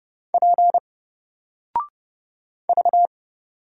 Morse code, heard as a keyed tone, about 30 wpm, 700 Hz. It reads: P V
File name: Single-Letter-Number-30wpm